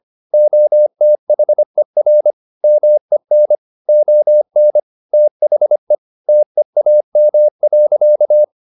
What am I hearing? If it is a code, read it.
Morse code, heard as a keyed tone, about 25 wpm, 600 Hz. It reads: OTHER MEN ON THE TEAM.
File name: the old man and the sea part 1 25wpm